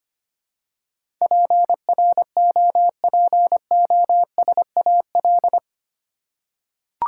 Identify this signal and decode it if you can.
Morse code, heard as a keyed tone, about 25 wpm, 700 Hz. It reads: PROPOSAL
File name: 8 Letter Words - First 3 Letters Given - 25wpm